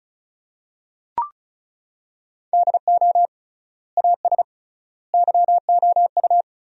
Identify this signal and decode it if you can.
Morse code, heard as a keyed tone, about 34 wpm, 700 Hz. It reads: DO AS YOU
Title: 3x Word-Spacing - Sentences from Top 100 Words - 35wpm